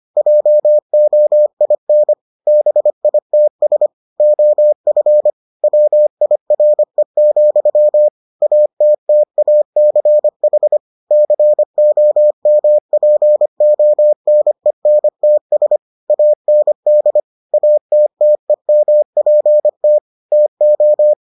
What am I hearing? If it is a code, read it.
Morse code, heard as a keyed tone, about 25 wpm, 600 Hz. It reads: JOIN BITS OF WIRE, ATTACH COMPONENTS AND ATTEMPT TO